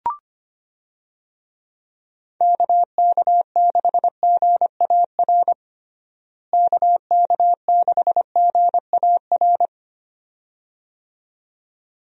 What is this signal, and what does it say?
Morse code, heard as a keyed tone, about 25 wpm, 700 Hz. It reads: KK6GAR KK6GAR
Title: USA Call Signs - 6 Characters - Repeated 2x - 25wpm